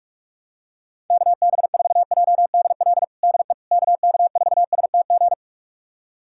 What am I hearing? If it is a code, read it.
Morse code, heard as a keyed tone, about 43 wpm, 700 Hz. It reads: KB4JDL DE KK4STG
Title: USA Callsign DE Callsign 45wpm